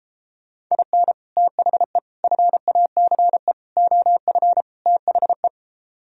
Morse code, heard as a keyed tone, about 33 wpm, 700 Hz. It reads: IN THE FACE OF THE
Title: Speed-Racing - Sets of 5 Words 28wpm